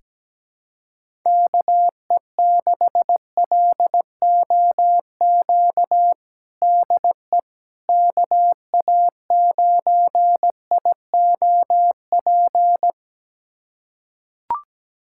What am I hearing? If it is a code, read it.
Morse code, heard as a keyed tone, about 17 wpm, 700 Hz. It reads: KE6LOQ DE KA9IOP
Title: USA Callsign DE Callsign 17wpm